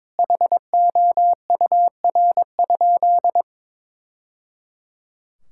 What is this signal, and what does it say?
Morse code, heard as a keyed tone, about 22 wpm, 700 Hz. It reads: HOUR?
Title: Sentences from Top 1000 Words - Review Part 5 - 22wpm